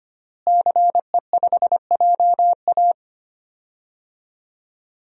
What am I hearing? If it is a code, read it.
Morse code, heard as a keyed tone, about 25 wpm, 700 Hz. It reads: CE5JA
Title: DXCC-100 Call Signs - 5 Characters - Repeated 3x - 25wpm